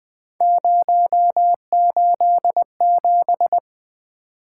Morse code, heard as a keyed tone, about 20 wpm, 700 Hz. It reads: 087
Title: Three Letter-Number Combinations - 20wpm